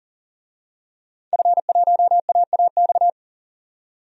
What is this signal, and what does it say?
Morse code, heard as a keyed tone, about 40 wpm, 700 Hz. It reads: F1AAX
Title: DXCC-100 Call Signs - 5 Characters - 40wpm